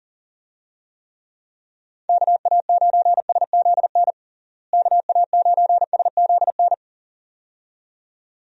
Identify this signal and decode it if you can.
Morse code, heard as a keyed tone, about 40 wpm, 700 Hz. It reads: KA9SZN KA9SZN
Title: USA Call Signs - 6 Characters - Repeated 2x - 40wpm